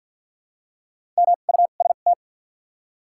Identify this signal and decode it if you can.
Morse code, heard as a keyed tone, about 47 wpm, 700 Hz. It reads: M U S T
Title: 2x Character Spacing - Top 500 Words - 50wpm